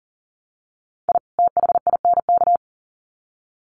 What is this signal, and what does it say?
Morse code, heard as a keyed tone, about 40 wpm, 700 Hz. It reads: I THINK
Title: Sets of 2 Words 40wpm